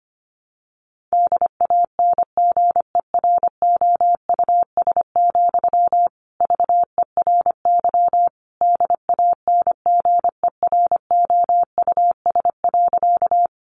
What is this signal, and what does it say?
Morse code, heard as a keyed tone, about 25 wpm, 700 Hz. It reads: DANGEROUS, VERY DANGEROUS.
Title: Sentences from Top 1200 Words 25wpm